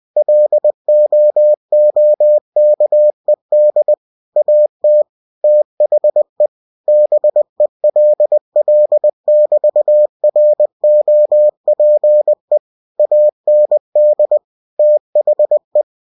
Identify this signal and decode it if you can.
Morse code, heard as a keyed tone, about 20 wpm, 600 Hz. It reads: LOOKED AT THE BELL<BT>ROPE AND THE
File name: cw-books / winnie-the-pooh / winnie-20wpm-page-0031